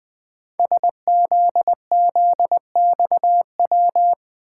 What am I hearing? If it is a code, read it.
Morse code, heard as a keyed tone, about 20 wpm, 700 Hz. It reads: SZZXW
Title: Code Groups - 5 Letters - 20wpm